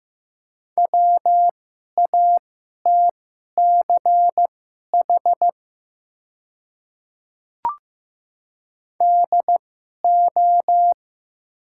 Morse code, heard as a keyed tone, about 15 wpm, spaced as for 11 wpm, 700 Hz. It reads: WATCH DO
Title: 2x Character Spacing - Top 500 Words - 15wpm